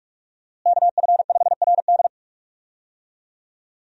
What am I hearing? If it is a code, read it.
Morse code, heard as a keyed tone, about 47 wpm, 700 Hz. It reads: KF5RD
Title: USA Call Signs - 5 Characters - Repeated 2x - 45wpm